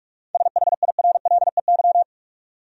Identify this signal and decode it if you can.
Morse code, heard as a keyed tone, about 46 wpm, 700 Hz. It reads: SHIRLEY
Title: USA Common Names - 45wpm